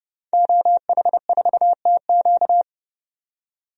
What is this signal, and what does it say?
Morse code, heard as a keyed tone, about 29 wpm, 700 Hz. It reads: OH4TQ
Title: DXCC-100 Call Signs - 5 Characters - 30wpm